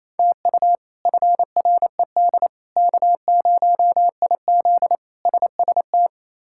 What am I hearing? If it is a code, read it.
Morse code, heard as a keyed tone, about 27 wpm, 700 Hz. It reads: TU FRED K0IZ SST
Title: K1USN SST QSO Practice - 28wpm